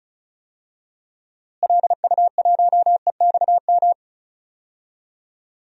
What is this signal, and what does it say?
Morse code, heard as a keyed tone, about 36 wpm, 700 Hz. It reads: LU1EXM
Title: DXCC-100 Call Signs - 6 Characters - Repeated 3x - 35wpm